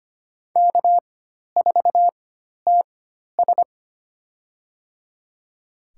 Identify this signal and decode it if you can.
Morse code, heard as a keyed tone, about 25 wpm, spaced as for 12 wpm, 700 Hz. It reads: K4TS
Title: USA Call Signs - 4 Characters - 4x Character Spacing - 25wpm